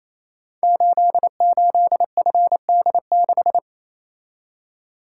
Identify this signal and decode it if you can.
Morse code, heard as a keyed tone, about 28 wpm, 700 Hz. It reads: 88FD6
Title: Code Groups - 5 Letter:Numbers - 28wpm